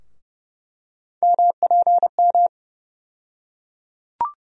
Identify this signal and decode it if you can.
Morse code, heard as a keyed tone, about 32 wpm, 700 Hz. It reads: MPM
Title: USA Call Sign Postfixes - 3 Letters 30wpm